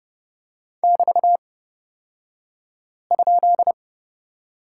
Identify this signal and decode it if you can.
Morse code, heard as a keyed tone, about 30 wpm, spaced as for 5 wpm, 700 Hz. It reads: <BT>?
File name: course / 30:10wpm / 235 Lesson 38d - Letter q - Two Characters 30wpm